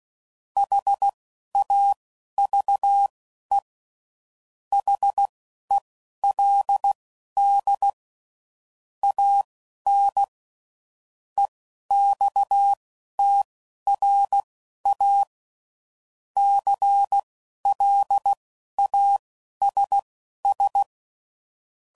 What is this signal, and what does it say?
Morse code, heard as a keyed tone, about 16 wpm, spaced as for 11 wpm, 800 Hz. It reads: HAVE HELD AN EXTRA CLASS
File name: CW Code Groups @10WPM / QSOs / QSO kc7haa